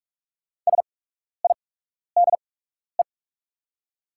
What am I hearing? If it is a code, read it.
Morse code, heard as a keyed tone, about 46 wpm, spaced as for 12 wpm, 700 Hz. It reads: SIDE
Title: 8x Character Spacing - Top 200 Words - 45wpm